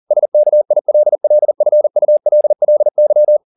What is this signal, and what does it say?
Morse code, heard as a keyed tone, about 40 wpm, 600 Hz. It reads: SKILLFULLY